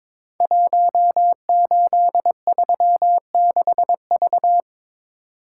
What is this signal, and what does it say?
Morse code, heard as a keyed tone, about 22 wpm, 700 Hz. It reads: 1836V
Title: Code Groups - 5 Letter:Numbers - 22wpm